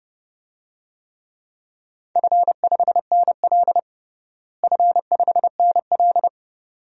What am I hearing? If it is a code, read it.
Morse code, heard as a keyed tone, about 32 wpm, 700 Hz. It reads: F5NL F5NL
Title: DXCC-100 Call Signs - 4 Characters - Repeated 2x - 30wpm